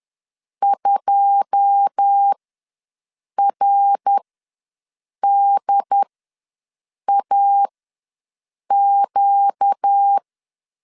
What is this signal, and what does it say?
Morse code, heard as a keyed tone, about 10 wpm, spaced as for 6 wpm, 800 Hz. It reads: 2RDAQ
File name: LEZIONI / 9 2 ws 10wpm 4130